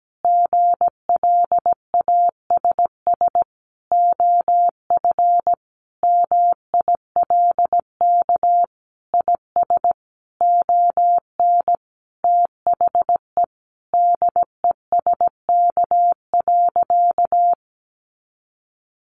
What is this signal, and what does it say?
Morse code, heard as a keyed tone, about 17 wpm, 700 Hz. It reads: GLASS OF MILK IS ON THE DESK.
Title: Additional Easy Phrases 17wpm